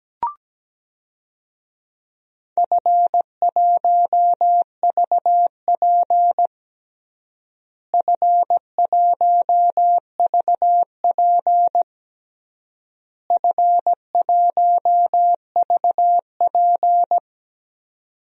Morse code, heard as a keyed tone, about 17 wpm, 700 Hz. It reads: F1VP F1VP F1VP
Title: DXCC-100 Call Signs - 4 Characters - Repeated 3x - 17wpm